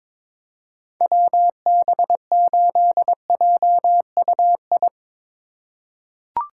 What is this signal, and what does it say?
Morse code, heard as a keyed tone, about 21 wpm, 700 Hz. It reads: WB8JUI
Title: K1USN SST QSO Practice - 22wpm